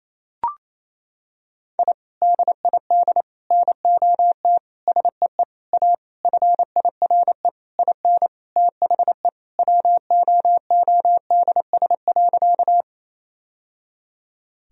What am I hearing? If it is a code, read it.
Morse code, heard as a keyed tone, about 28 wpm, 700 Hz. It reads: I DID NOT SEE A FIRE IN THE WOODS.
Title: Sentences from Top 1000 Words - Review Part 2 - 28wpm